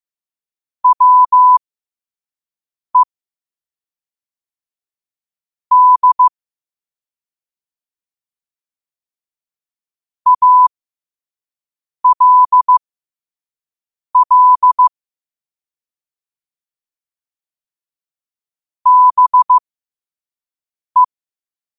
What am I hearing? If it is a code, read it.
Morse code, heard as a keyed tone, about 15 wpm, spaced as for 5 wpm, 1000 Hz. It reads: WE D ALL BE
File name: qotdfiles / 5wpm